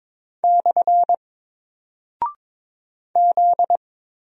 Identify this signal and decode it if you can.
Morse code, heard as a keyed tone, about 22 wpm, 700 Hz. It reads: / Z
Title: Single Character (Letter:Number:Punctuation:Prosign) Alt 22wpm